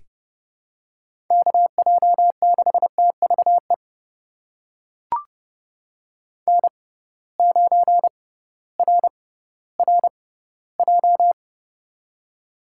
Morse code, heard as a keyed tone, about 32 wpm, spaced as for 17 wpm, 700 Hz. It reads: KJ6TVE N9RRJ
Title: USA Call Signs - 6x Character Spacing - 30wpm